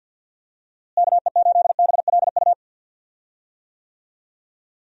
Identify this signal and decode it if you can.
Morse code, heard as a keyed tone, about 53 wpm, 700 Hz. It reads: KE8BLU
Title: USA Call Signs - 6 Characters - Repeated 3x - 50wpm